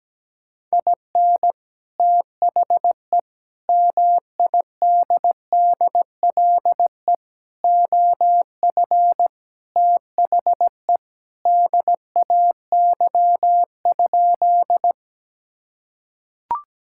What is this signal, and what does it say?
Morse code, heard as a keyed tone, about 17 wpm, 700 Hz. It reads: IN THE MIDDLE OF THE DAY?
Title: Sentences from Top 1000 Words - Review Part 6 - 17wpm